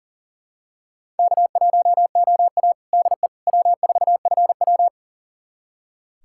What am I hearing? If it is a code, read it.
Morse code, heard as a keyed tone, about 40 wpm, 700 Hz. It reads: K1OA DE W4FW
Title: USA Callsign DE Callsign - 4 Characters - 40wpm